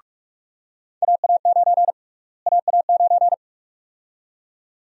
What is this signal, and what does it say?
Morse code, heard as a keyed tone, about 46 wpm, 700 Hz. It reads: AA9 AA9
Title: USA Call Sign Prefixes - Repeated 2x - 45wpm